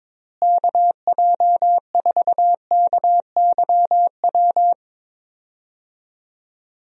Morse code, heard as a keyed tone, about 22 wpm, 700 Hz. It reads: KJ4KYW
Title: USA Call Signs - 6 Characters - Repeated 3x - 22wpm